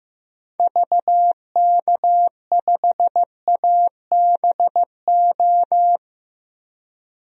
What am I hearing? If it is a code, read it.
Morse code, heard as a keyed tone, about 15 wpm, 700 Hz. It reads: VK5ABO
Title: DXCC-100 Call Signs - 6 Characters - 15wpm